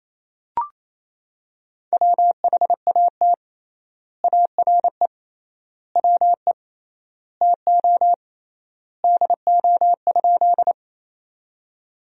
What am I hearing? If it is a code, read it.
Morse code, heard as a keyed tone, about 28 wpm, 700 Hz. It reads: WHAT ARE WE TO DO?
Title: 3x Word-Spacing - Sentences from Top 100 Words - 28wpm